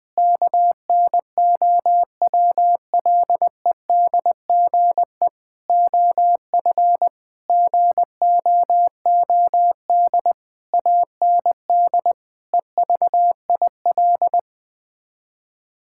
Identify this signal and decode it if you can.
Morse code, heard as a keyed tone, about 20 wpm, 700 Hz. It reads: KNOWLEDGE OF GOOD AND EVIL
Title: Sets of 9 Words - 20wpm